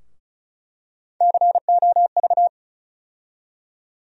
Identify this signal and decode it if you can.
Morse code, heard as a keyed tone, about 36 wpm, 700 Hz. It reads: COV
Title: USA Call Sign Postfixes - 3 Letters 35wpm